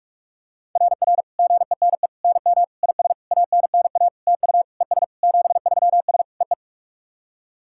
Morse code, heard as a keyed tone, about 46 wpm, 690 Hz. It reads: RR GENE NM IS ANNA TU ES 73S EE
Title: SOTA QSO Practice 45wpm